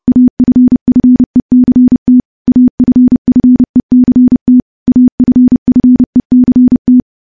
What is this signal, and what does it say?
Morse code, heard as a keyed tone, about 28 wpm, 250 Hz. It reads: AFFECT AFFECT AFFECT
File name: morse-training-30WPM